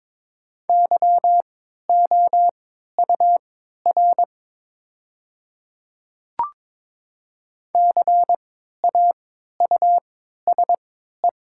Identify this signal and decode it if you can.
Morse code, heard as a keyed tone, about 22 wpm, spaced as for 12 wpm, 700 Hz. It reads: YOUR CAUSE
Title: 3x Character Spacing - Top 200 Words - 22wpm